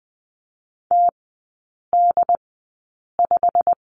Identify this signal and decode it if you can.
Morse code, heard as a keyed tone, about 20 wpm, spaced as for 8 wpm, 700 Hz. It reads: TD5
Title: 087 Lesson 15e - Number 5 - Three Characters 20wpm